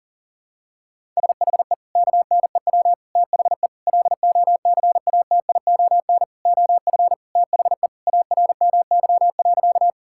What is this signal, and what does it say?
Morse code, heard as a keyed tone, about 40 wpm, 700 Hz. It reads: SHE KNEW THE LOCATION OF THE ARMY.
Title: Sentences from Top 1000 Words - Review Part 6 - 40wpm